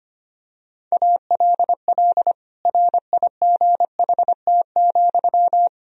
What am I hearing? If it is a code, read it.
Morse code, heard as a keyed tone, about 25 wpm, 700 Hz. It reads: ALL RIGHT,
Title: Sentences from Top 1000 Words - Review Part 3 - 25wpm